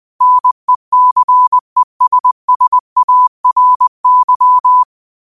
Morse code, heard as a keyed tone, about 20 wpm, 1000 Hz. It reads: NECESSARY
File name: 20wpm